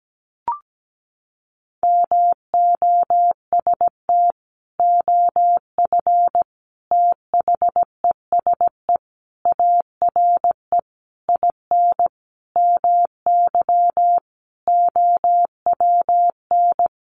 Morse code, heard as a keyed tone, about 17 wpm, 700 Hz. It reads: MOST OF THESE ARE IN MY OWN
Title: Sentences from Top 1200 Words 17wpm